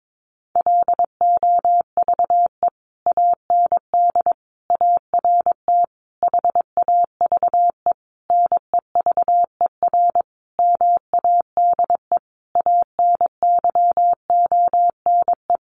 Morse code, heard as a keyed tone, about 22 wpm, 700 Hz. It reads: LOVE AND ART HAVE NEVER MADE ANYONE
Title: Sentences from Top 1100 Words 22wpm